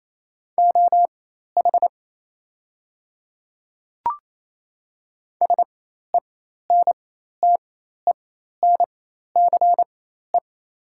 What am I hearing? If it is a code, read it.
Morse code, heard as a keyed tone, about 28 wpm, spaced as for 13 wpm, 700 Hz. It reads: OH SENTENCE
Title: 4x Character Spacing - Top 500 Words - 28wpm